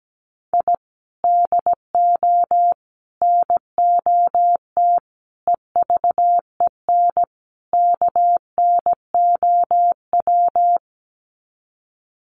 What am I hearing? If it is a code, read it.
Morse code, heard as a keyed tone, about 17 wpm, 700 Hz. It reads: I DO NOT EVEN KNOW
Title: Sets of 5 Words 17wpm